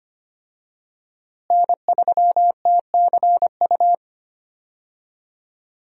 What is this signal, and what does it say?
Morse code, heard as a keyed tone, about 25 wpm, 700 Hz. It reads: N3TCU
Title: USA Call Signs - 5 Characters - Repeated 2x - 25wpm